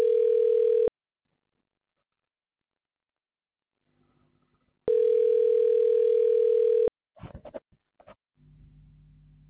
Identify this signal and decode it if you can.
Ringback tone (North America), 440 + 480 Hz.
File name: recordings / RE3982a7a07b5a92ad8eda7f5a1f10056e